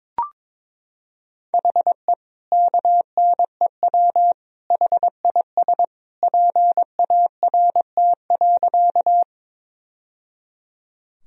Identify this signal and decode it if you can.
Morse code, heard as a keyed tone, about 21 wpm, 700 Hz. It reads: HE KNEW HIS PART.
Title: Sentences from Top 1000 Words - Review Part 1 - 22wpm